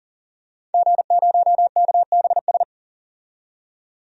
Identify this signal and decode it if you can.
Morse code, heard as a keyed tone, about 40 wpm, 700 Hz. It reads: G0KBS
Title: DXCC-100 Call Signs - 5 Characters - 40wpm